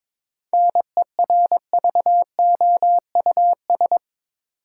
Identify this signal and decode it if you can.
Morse code, heard as a keyed tone, about 22 wpm, 700 Hz. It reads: NERVOUS
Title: Seven Letter Words - 22wpm